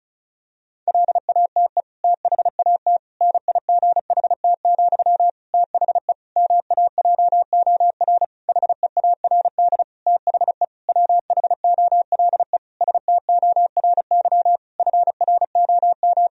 Morse code, heard as a keyed tone, about 36 wpm, 700 Hz. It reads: LATE THAT NIGHT, THE MAJOR HEARD THE WHOLE STORY FROM
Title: Sentences from Top 1000 Words - Review Part 4 - 35wpm